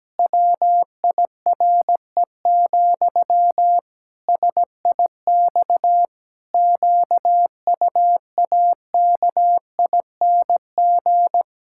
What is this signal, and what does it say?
Morse code, heard as a keyed tone, about 17 wpm, 700 Hz. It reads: WIRE, SIX QUAKING
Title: Pangram Challenge v2 - 17wpm